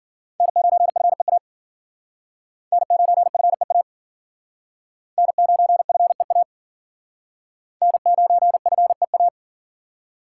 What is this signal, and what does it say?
Morse code, heard as a keyed tone, about 53 wpm, 700 Hz. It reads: N9FEA N9FEA N9FEA N9FEA